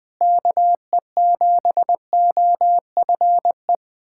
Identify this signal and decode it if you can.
Morse code, heard as a keyed tone, about 20 wpm, 700 Hz. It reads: KE7OFE
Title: USA Callsign DE Callsign - 6 Characters - 20wpm